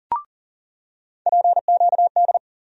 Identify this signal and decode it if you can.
Morse code, heard as a keyed tone, about 40 wpm, 700 Hz. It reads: PQD
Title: USA Call Sign Postfixes - 3 Letters 40wpm